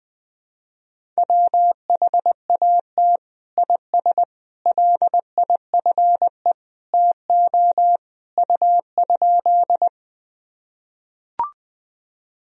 Morse code, heard as a keyed tone, about 20 wpm, 700 Hz. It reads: WHAT IS LIFE TO U?
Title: Sentences from Top 300 Words and Abbreviations 20wpm